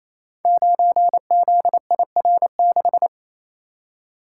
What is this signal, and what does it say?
Morse code, heard as a keyed tone, about 28 wpm, 700 Hz. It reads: 9ZIR6
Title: Code Groups - 5 Letter:Numbers - 28wpm